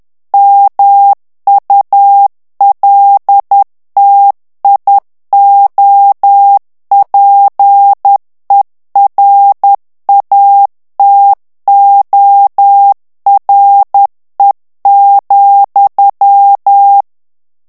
Morse code, heard as a keyed tone, about 10 wpm, 800 Hz. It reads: MULTIOPERATORE,
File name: ITALIEN texte clair 2 vitesse 10 mots minute